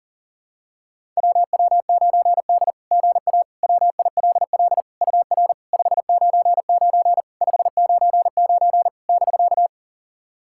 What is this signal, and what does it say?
Morse code, heard as a keyed tone, about 40 wpm, 690 Hz. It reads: WW9D GA WILL UR 599 599 <BK>
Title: SOTA QSO Practice 40wpm